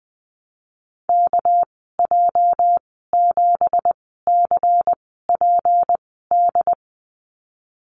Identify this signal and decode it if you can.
Morse code, heard as a keyed tone, about 19 wpm, spaced as for 14 wpm, 700 Hz. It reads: KJ7CPD
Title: 231 Lesson 37g - Letter x - Call Signs 20wpm